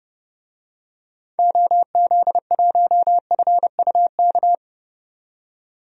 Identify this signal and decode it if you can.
Morse code, heard as a keyed tone, about 30 wpm, 700 Hz. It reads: OZ1FUK
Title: DXCC-100 Call Signs - 6 Characters - 30wpm